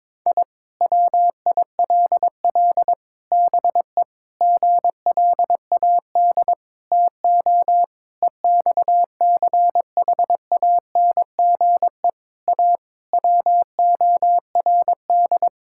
Morse code, heard as a keyed tone, about 22 wpm, 700 Hz. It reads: I WILL BE GLAD TO EXCHANGE A WORD
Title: Sentences from Top 1000 Words - Review Part 6 - 22wpm